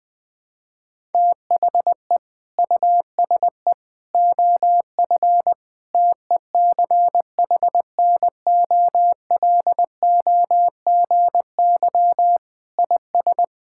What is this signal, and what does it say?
Morse code, heard as a keyed tone, about 20 wpm, 700 Hz. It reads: THE USE OF TECHNOLOGY IS
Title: Sentences with 2-Word Phrases - 20wpm